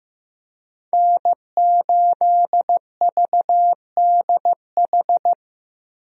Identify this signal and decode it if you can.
Morse code, heard as a keyed tone, about 15 wpm, 700 Hz. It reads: N8VDH
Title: USA Call Signs - 5 Characters - 15wpm